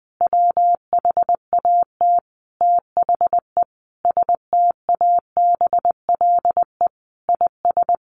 Morse code, heard as a keyed tone, about 20 wpm, 700 Hz. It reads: WHAT THE STABLE IS
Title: Sentences from Top 1500 Words 20wpm